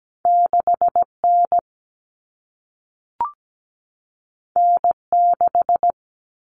Morse code, heard as a keyed tone, about 17 wpm, 700 Hz. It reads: 6N N6
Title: Common Cut Numbers - 2-Digit 17wpm